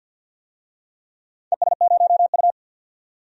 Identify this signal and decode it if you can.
Morse code, heard as a keyed tone, about 51 wpm, 700 Hz. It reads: ES0U
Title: DXCC-100 Call Signs - 4 Characters - 50wpm